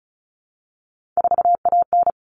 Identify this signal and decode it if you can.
Morse code, heard as a keyed tone, about 34 wpm, 700 Hz. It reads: 4AN
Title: All Cut Numbers - 3-Digit 35wpm